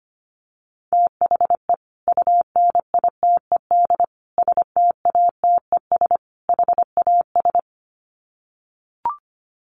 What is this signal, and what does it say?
Morse code, heard as a keyed tone, about 25 wpm, 700 Hz. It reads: THE UNITED STATES HAS
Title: Sets of 4 Words 25wpm